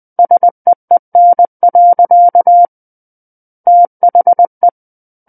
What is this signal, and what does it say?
Morse code, heard as a keyed tone, about 20 wpm, 700 Hz. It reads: SEEN. THE
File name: 20 WPM